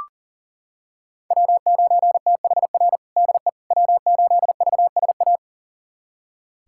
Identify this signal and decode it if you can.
Morse code, heard as a keyed tone, about 40 wpm, 700 Hz. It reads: W9THR DE W8VSA
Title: USA Callsign DE Callsign - 5 Characters - 40wpm